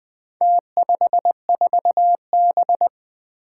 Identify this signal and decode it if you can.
Morse code, heard as a keyed tone, about 20 wpm, 700 Hz. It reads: T54B
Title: Four Challenging Letter-Number Combinations 20wpm